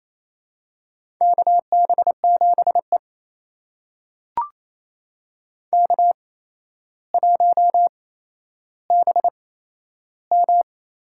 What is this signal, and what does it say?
Morse code, heard as a keyed tone, about 28 wpm, spaced as for 12 wpm, 700 Hz. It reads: KB7E K1BM
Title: USA Call Signs - 8x Character Spacing - 28wpm